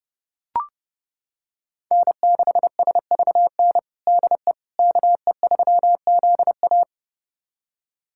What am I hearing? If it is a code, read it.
Morse code, heard as a keyed tone, about 30 wpm, 700 Hz. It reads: N6SVN DE KE3ZA
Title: USA Callsign DE Callsign - 5 Characters - 30wpm